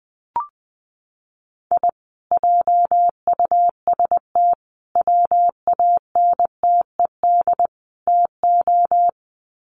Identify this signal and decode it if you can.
Morse code, heard as a keyed tone, about 20 wpm, 700 Hz. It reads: I JUST WANTED TO
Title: Sets of 4 Words 20wpm